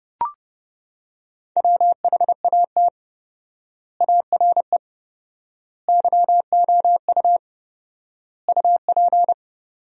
Morse code, heard as a keyed tone, about 30 wpm, 700 Hz. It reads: WHAT ARE YOU UP
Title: 4x Word-Spacing - Sentences from Top 100 Words - 30wpm